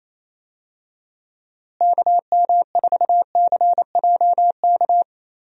Morse code, heard as a keyed tone, about 28 wpm, 700 Hz. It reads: KM4CJK
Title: USA Call Signs - 6 Characters - Repeated 3x - 28wpm